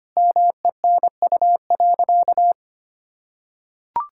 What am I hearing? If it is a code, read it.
Morse code, heard as a keyed tone, about 25 wpm, 700 Hz. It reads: MENU.
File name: practice / 25wpm / Sentences with 4-Word Phrases - 25wpm